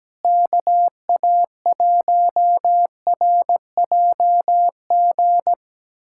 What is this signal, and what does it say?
Morse code, heard as a keyed tone, about 17 wpm, 700 Hz. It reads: KA1RJG
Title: USA Callsign DE Callsign 17wpm